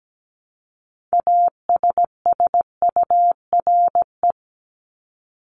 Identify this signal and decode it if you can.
Morse code, heard as a keyed tone, about 17 wpm, 700 Hz. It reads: ASSURE
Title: Top 1501-1750 Words 17wpm